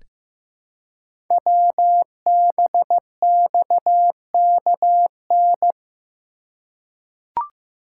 Morse code, heard as a keyed tone, about 15 wpm, 700 Hz. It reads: WBXKN
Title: Code Groups - 5 Letters - 15wpm